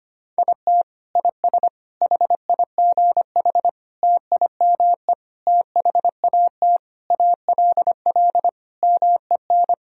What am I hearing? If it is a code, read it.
Morse code, heard as a keyed tone, about 25 wpm, 700 Hz. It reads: IT IS HIGH TIME THAT ALL MEN
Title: Sentences from Top 1000 Words - Review Part 1 - 25wpm